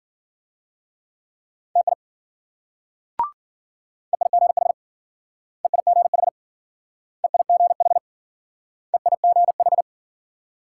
Morse code, heard as a keyed tone, about 52 wpm, 700 Hz. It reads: TI EIGH EIGH EIGH EIGH